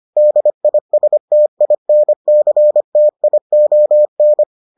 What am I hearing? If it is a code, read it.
Morse code, heard as a keyed tone, about 25 wpm, 600 Hz. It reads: DISTINCTION